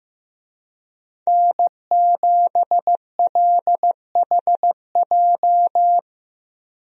Morse code, heard as a keyed tone, about 16 wpm, 700 Hz. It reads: N7LHJ
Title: USA Call Signs - 5 Characters - 15wpm